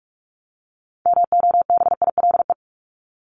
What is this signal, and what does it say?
Morse code, heard as a keyed tone, about 44 wpm, 700 Hz. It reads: MOBILE
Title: Top 1751-2000 Words 45wpm